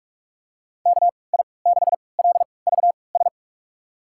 Morse code, heard as a keyed tone, about 45 wpm, spaced as for 25 wpm, 700 Hz. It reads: KI6LVS
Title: USA Call Signs - 3x Character Spacing - 45wpm